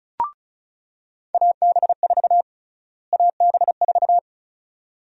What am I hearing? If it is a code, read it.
Morse code, heard as a keyed tone, about 34 wpm, 700 Hz. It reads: AB4 AB4
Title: USA Call Sign Prefixes - Repeated 2x - 35wpm